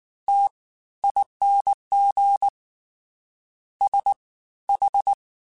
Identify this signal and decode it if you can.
Morse code, heard as a keyed tone, about 19 wpm, spaced as for 14 wpm, 800 Hz. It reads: TING SH